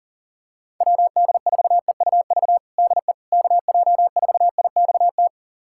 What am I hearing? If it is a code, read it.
Morse code, heard as a keyed tone, about 37 wpm, 700 Hz. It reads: WD4EUV DE KJ4IXT
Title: USA Callsign DE Callsign - 6 Characters - 40wpm